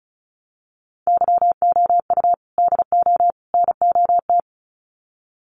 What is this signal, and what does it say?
Morse code, heard as a keyed tone, about 34 wpm, 700 Hz. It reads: YOU DO NOT
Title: Sets of 3 Words 35wpm